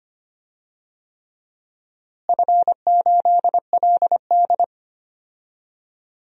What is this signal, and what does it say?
Morse code, heard as a keyed tone, about 25 wpm, 700 Hz. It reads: F8LD
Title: DXCC-100 Call Signs - 4 Characters - Repeated 3x - 25wpm